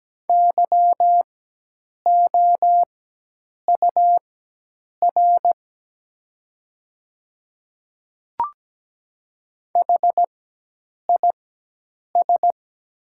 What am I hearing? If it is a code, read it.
Morse code, heard as a keyed tone, about 17 wpm, spaced as for 8 wpm, 700 Hz. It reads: YOUR HIS
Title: 4x Character Spacing - Top 100 Words - 17wpm